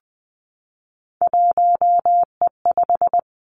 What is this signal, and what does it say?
Morse code, heard as a keyed tone, about 20 wpm, 700 Hz. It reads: 1E5
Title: Three Challenging Letter-Number Combinations 20wpm